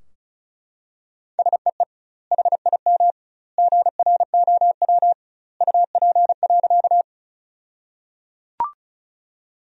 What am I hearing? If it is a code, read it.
Morse code, heard as a keyed tone, about 36 wpm, 700 Hz. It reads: SEE HIM GROW UP.
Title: 2x Word-Spacing - Sentences from Top 300 Words - 35wpm